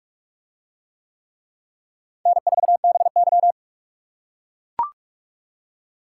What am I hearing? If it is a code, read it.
Morse code, heard as a keyed tone, about 42 wpm, 700 Hz. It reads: N4BY